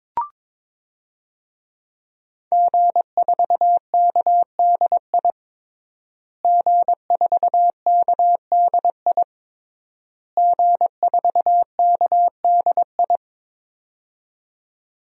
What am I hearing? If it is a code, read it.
Morse code, heard as a keyed tone, about 22 wpm, 700 Hz. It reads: G4KDI G4KDI G4KDI
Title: DXCC-100 Call Signs - 5 Characters - Repeated 3x - 22wpm